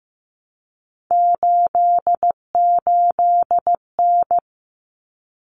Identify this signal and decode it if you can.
Morse code, heard as a keyed tone, about 15 wpm, 700 Hz. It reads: 88N
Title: Common Cut Numbers - 3-Digit 15wpm